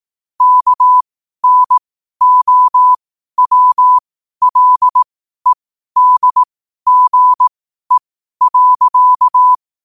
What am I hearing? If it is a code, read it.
Morse code, heard as a keyed tone, about 18 wpm, spaced as for 12 wpm, 1000 Hz. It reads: KNOWLEDGE.
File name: qotdfiles / 12wpm